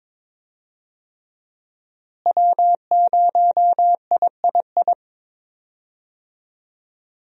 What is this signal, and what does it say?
Morse code, heard as a keyed tone, about 22 wpm, 700 Hz. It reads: W0III
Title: USA Call Signs - 5 Characters - Repeated 3x - 22wpm